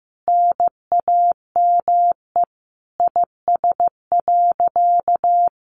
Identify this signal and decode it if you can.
Morse code, heard as a keyed tone, about 15 wpm, 700 Hz. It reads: NAME IS.
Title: Sentences from Top 1100 Words 15wpm